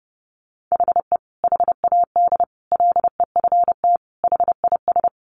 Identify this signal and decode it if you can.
Morse code, heard as a keyed tone, about 30 wpm, 700 Hz. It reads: HE HAD LEFT HIS
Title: Sentences from Top 1500 Words 30wpm